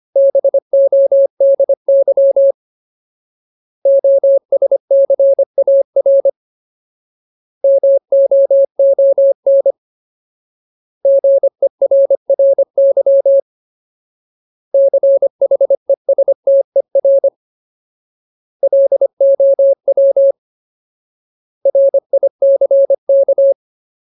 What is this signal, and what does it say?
Morse code, heard as a keyed tone, about 25 wpm, 550 Hz. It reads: BODY OSCAR MOON GERRY CHESTER LOW RICK
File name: Ultimate Word List / 25WPM Ultimate